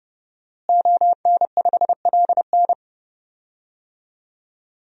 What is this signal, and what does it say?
Morse code, heard as a keyed tone, about 30 wpm, 700 Hz. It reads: ON5LN
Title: DXCC-100 Call Signs - 5 Characters - Repeated 2x - 30wpm